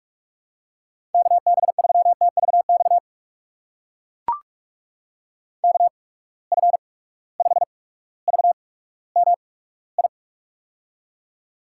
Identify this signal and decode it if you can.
Morse code, heard as a keyed tone, about 44 wpm, spaced as for 19 wpm, 700 Hz. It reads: KB3TVX KF5VMI
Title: USA Call Signs - 6 Characters - 8x Character Spacing - 45wpm